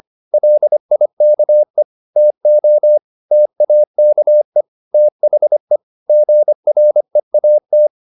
Morse code, heard as a keyed tone, about 25 wpm, 600 Hz. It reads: LIKE TO TAKE THE GREAT
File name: the old man and the sea part 1 25wpm